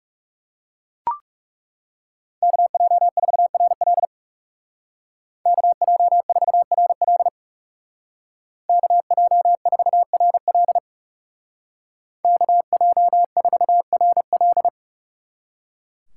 Morse code, heard as a keyed tone, about 38 wpm, 700 Hz. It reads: KJ4RL KJ4RL KJ4RL KJ4RL